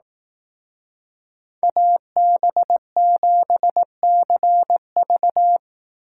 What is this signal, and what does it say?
Morse code, heard as a keyed tone, about 19 wpm, 700 Hz. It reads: AB7CV
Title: Speed Racing - USA Call Signs 15wpm